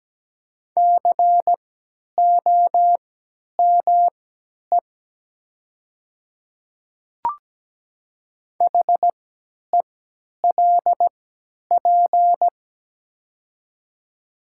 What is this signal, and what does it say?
Morse code, heard as a keyed tone, about 17 wpm, spaced as for 10 wpm, 700 Hz. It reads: COME HELP
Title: 3x Character Spacing - Top 300 Words - 17wpm